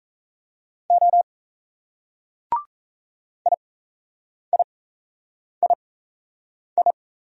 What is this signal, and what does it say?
Morse code, heard as a keyed tone, about 35 wpm, spaced as for 8 wpm, 700 Hz. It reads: O IIII
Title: Speed-Racing Single Letter-Number 28wpm